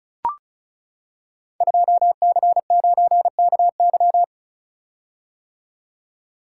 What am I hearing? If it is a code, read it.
Morse code, heard as a keyed tone, about 36 wpm, 700 Hz. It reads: 2C9KY
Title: Code Groups - 5 Letter:Numbers - 35wpm